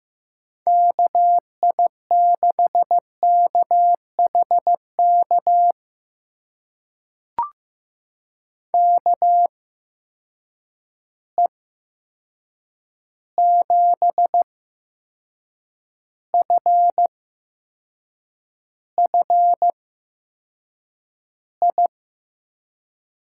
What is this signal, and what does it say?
Morse code, heard as a keyed tone, about 15 wpm, spaced as for 6 wpm, 700 Hz. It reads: KI6KHK KE7FFI
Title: USA Call Signs - 6 Characters - 8x Character Spacing - 15wpm